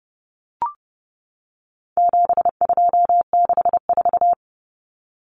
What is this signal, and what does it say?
Morse code, heard as a keyed tone, about 30 wpm, 700 Hz. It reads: 7264
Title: Numbers - 4-Digit 30wpm